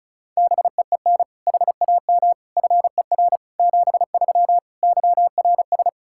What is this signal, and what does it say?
Morse code, heard as a keyed tone, about 34 wpm, 700 Hz. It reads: BEEN HAM FER 73 YRS
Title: Practice QSO - Exchange 3 - More Info - 35wpm